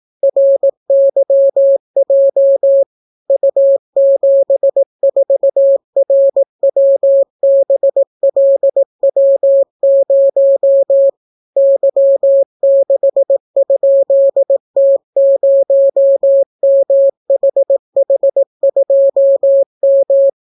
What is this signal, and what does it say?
Morse code, heard as a keyed tone, about 18 wpm, 550 Hz. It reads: RYJ U74RWBLW0 Y6?T0MHH2M